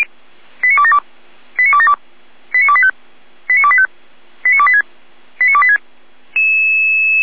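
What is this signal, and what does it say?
Five-tone selective call (ZVEI), heard as quick runs of five tones, tones of about 70 ms. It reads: 89272 89272 89285 89285 89286 89286
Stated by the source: ZVEI five-tone selective call received on an SDR.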